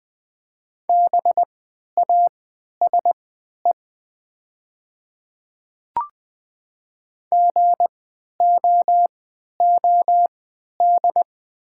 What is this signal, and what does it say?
Morse code, heard as a keyed tone, about 21 wpm, spaced as for 11 wpm, 700 Hz. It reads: BASE GOOD
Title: 3x Character Spacing - Top 300 Words - 20wpm